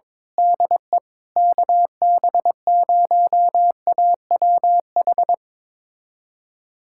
Morse code, heard as a keyed tone, about 21 wpm, 700 Hz. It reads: DE KB0AWH
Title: USA Callsign DE Callsign - 6 Characters - 22wpm